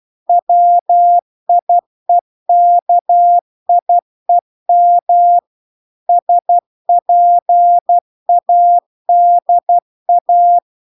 Morse code, heard as a keyed tone, about 12 wpm, 700 Hz. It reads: WIEKIEM SPADA